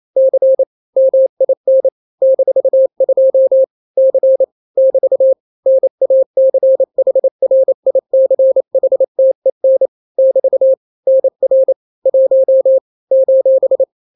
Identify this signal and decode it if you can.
Morse code, heard as a keyed tone, about 28 wpm, 530 Hz. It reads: C MIN -2 C <BT> NACHRICHTEN <BT> NR 1 :
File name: daily / 27wpm